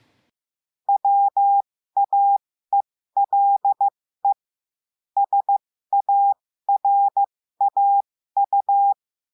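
Morse code, heard as a keyed tone, about 15 wpm, spaced as for 13 wpm, 800 Hz. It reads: WAELE SARAU
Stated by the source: Morse code lesson adding the letter W, with a five-letter-group copy drill.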